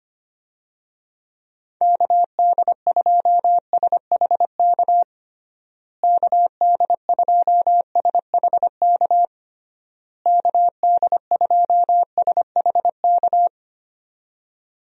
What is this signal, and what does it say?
Morse code, heard as a keyed tone, about 25 wpm, 700 Hz. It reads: KD2SHK KD2SHK KD2SHK
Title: USA Call Signs - 6 Characters - Repeated 3x - 25wpm